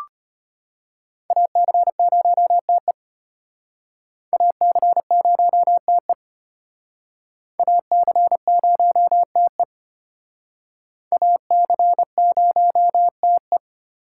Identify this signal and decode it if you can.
Morse code, heard as a keyed tone, about 32 wpm, 700 Hz. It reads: AC0TE AC0TE AC0TE AC0TE